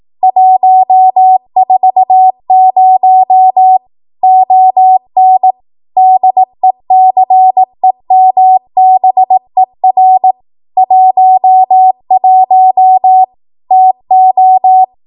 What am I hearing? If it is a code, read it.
Morse code, heard as a keyed tone, about 18 wpm, 750 Hz. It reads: 140 ON DECEMBER 11 TO
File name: Archive / 18 WPM / 18WPM